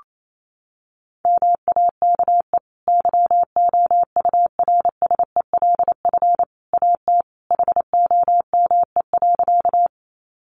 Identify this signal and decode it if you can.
Morse code, heard as a keyed tone, about 28 wpm, 700 Hz. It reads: MAKE YOURSELF AT HOME.
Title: Sentences from Top 600 Words 28wpm